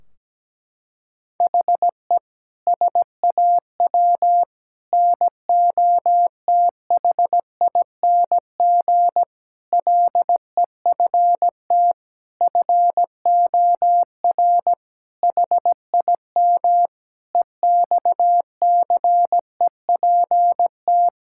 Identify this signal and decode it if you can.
Morse code, heard as a keyed tone, about 17 wpm, 700 Hz. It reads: HE SAW NOTHING LEFT FOR HIM EXCEPT
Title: Sentences from Top 1600 Words 17wpm